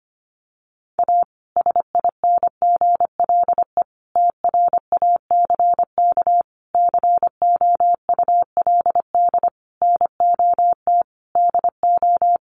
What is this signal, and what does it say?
Morse code, heard as a keyed tone, about 25 wpm, 700 Hz. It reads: A SINGLE TRACK COULD NOT DO
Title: Sentences from Top 1100 Words 25wpm